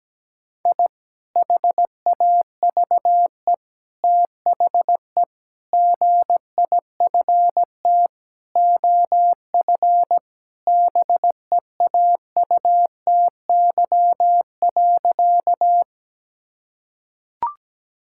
Morse code, heard as a keyed tone, about 17 wpm, 700 Hz. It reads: I HAVE THE GIFT OF BEAUTY.
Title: Sentences from Top 1000 Words - Review Part 6 - 17wpm